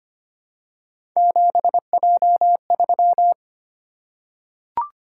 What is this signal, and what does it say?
Morse code, heard as a keyed tone, about 25 wpm, 700 Hz. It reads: 7J3
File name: Three Letter-Number Combinations - 25wpm